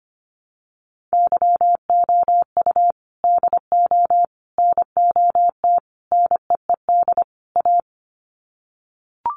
Morse code, heard as a keyed tone, about 25 wpm, 700 Hz. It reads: YOU DO NOT NEED A
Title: Sets of 5 Words Encore 25wpm